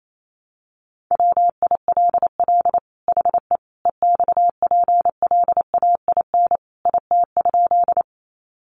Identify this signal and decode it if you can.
Morse code, heard as a keyed tone, about 28 wpm, 700 Hz. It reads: WILL HE EXPLAIN IT?
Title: Sentences from Top 1100 Words 28wpm